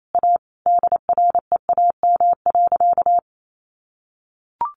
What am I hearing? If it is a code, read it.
Morse code, heard as a keyed tone, about 28 wpm, 700 Hz. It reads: A DREAM.
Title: Sentences from Top 1500 Words 28wpm